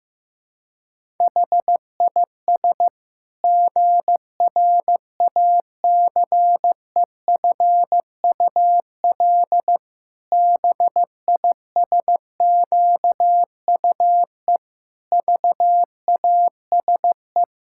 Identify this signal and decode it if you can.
Morse code, heard as a keyed tone, about 16 wpm, 700 Hz. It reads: HIS GRACEFUL BISQUE VASE
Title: Pangram Challenge v2 - 15wpm